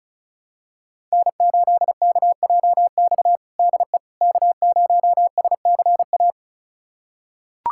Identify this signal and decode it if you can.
Morse code, heard as a keyed tone, about 36 wpm, 700 Hz. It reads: N8KJX DE K0SCA
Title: USA Callsign DE Callsign - 5 Characters - 35wpm